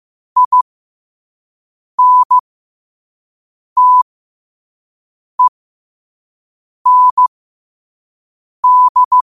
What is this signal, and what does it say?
Morse code, heard as a keyed tone, about 15 wpm, spaced as for 5 wpm, 1000 Hz. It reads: INTEND
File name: qotdfiles / 5wpm